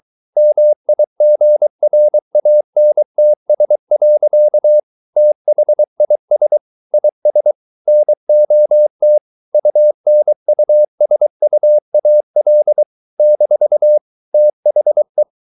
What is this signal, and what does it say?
Morse code, heard as a keyed tone, about 23 wpm, 600 Hz. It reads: MIGRANTS. THIS IS NOT UNUSUAL - THE